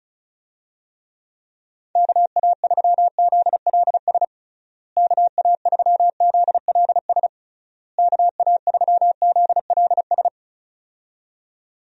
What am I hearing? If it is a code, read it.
Morse code, heard as a keyed tone, about 36 wpm, 700 Hz. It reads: KA3ZLS KA3ZLS KA3ZLS
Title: USA Call Signs - 6 Characters - Repeated 3x - 35wpm